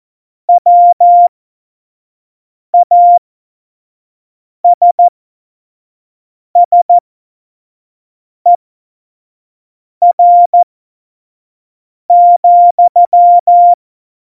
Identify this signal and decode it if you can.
Morse code, heard as a keyed tone, about 14 wpm, spaced as for 5 wpm, 700 Hz. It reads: WASSER,
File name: Morsekurs / Witze / Witz 06. Oktober 2020, 25 F70